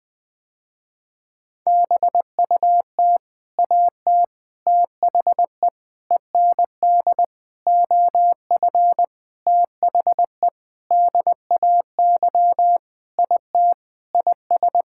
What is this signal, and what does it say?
Morse code, heard as a keyed tone, about 20 wpm, 700 Hz. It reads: BUT AT THE END OF THE DAY IT IS
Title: Sentences with 7-Word Phrases - Spoken Beginning - 20wpm